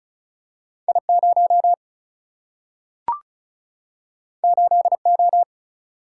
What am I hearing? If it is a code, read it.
Morse code, heard as a keyed tone, about 36 wpm, 700 Hz. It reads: I0 8O
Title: Single-Letter Single-Number 35wpm